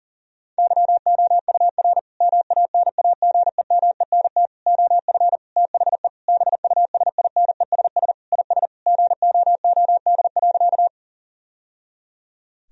Morse code, heard as a keyed tone, about 40 wpm, 700 Hz. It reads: YOUR MANAGEMENT OF THE BUSINESS IS GOOD.
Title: Sentences from Top 1000 Words - Review Part 4 - 40wpm